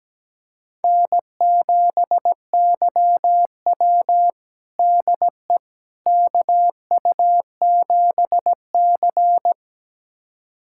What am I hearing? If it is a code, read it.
Morse code, heard as a keyed tone, about 17 wpm, 700 Hz. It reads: N7YW DE KU7C
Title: USA Callsign DE Callsign - 4 Characters - 17wpm